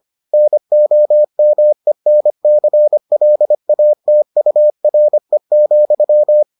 Morse code, heard as a keyed tone, about 25 wpm, 600 Hz. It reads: NOMENCLATURE,